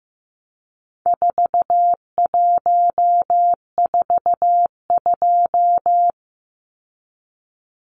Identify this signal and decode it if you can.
Morse code, heard as a keyed tone, about 15 wpm, 700 Hz. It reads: 4142
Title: Numbers - 4-Digit 15wpm